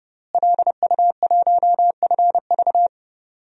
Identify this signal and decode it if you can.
Morse code, heard as a keyed tone, about 30 wpm, 700 Hz. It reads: LU1FV
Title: DXCC-100 Call Signs - 5 Characters - Repeated 3x - 30wpm